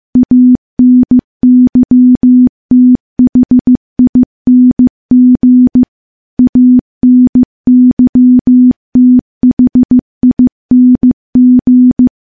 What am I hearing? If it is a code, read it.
Morse code, heard as a keyed tone, about 15 wpm, 250 Hz. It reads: ANYTHING ANYTHING
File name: morse / morse-training-15WPM